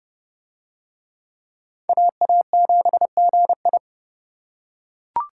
Morse code, heard as a keyed tone, about 30 wpm, 700 Hz. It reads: AA7GI